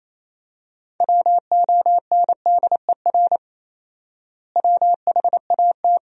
Morse code, heard as a keyed tone, about 27 wpm, 700 Hz. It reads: WONDER WHAT
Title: 4x Word-Spacing - Sentences from Top 500 Words - 28wpm